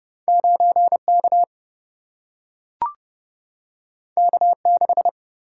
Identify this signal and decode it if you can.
Morse code, heard as a keyed tone, about 30 wpm, 700 Hz. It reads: 9K K6
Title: Single-Letter Single-Number 30wpm